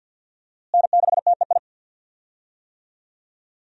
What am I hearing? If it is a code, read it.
Morse code, heard as a keyed tone, about 48 wpm, 700 Hz. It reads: N6TEI
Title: USA Call Signs - 5 Characters - Repeated 2x - 50wpm